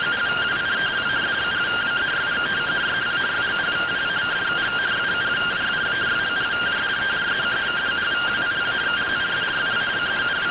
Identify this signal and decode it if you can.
Two-tone teleprinter signal, about 170 Hz shift.